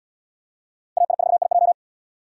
Morse code, heard as a keyed tone, about 68 wpm, 690 Hz. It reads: AI4IJ